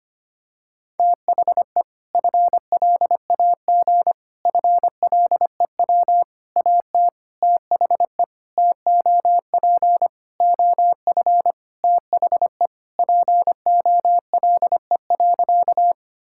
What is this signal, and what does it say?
Morse code, heard as a keyed tone, about 25 wpm, 700 Hz. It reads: THE FLAG FLEW AT THE TOP OF THE POLE.
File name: practice / 25wpm / Sentences with 4-Word Phrases - 25wpm